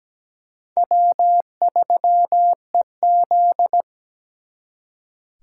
Morse code, heard as a keyed tone, about 20 wpm, 700 Hz. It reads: W3EZ
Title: Speed Racing - USA Call Signs 17wpm